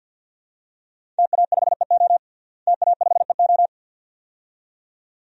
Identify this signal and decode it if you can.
Morse code, heard as a keyed tone, about 52 wpm, 700 Hz. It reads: TA5EO TA5EO
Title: DXCC-100 Call Signs - 5 Characters - Repeated 2x - 50wpm